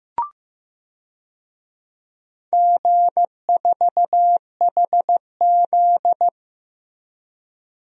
Morse code, heard as a keyed tone, about 15 wpm, 700 Hz. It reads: G4HZ
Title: DXCC-100 Call Signs - 4 Characters - Repeated 3x - 15wpm